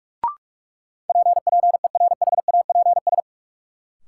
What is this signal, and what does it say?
Morse code, heard as a keyed tone, about 42 wpm, 700 Hz. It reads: PPERHAPS
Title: Sentences from Top 1100 Words - New Word Drill - 45wpm